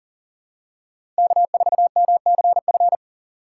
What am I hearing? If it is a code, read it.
Morse code, heard as a keyed tone, about 40 wpm, 700 Hz. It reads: K4MCF
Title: USA Call Signs - 5 Characters - 6x Character Spacing - 40wpm